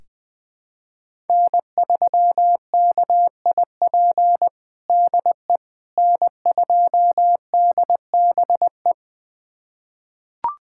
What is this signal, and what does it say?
Morse code, heard as a keyed tone, about 20 wpm, 700 Hz. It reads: N3KIP DE N2DBE
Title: USA Callsign DE Callsign - 5 Characters - 20wpm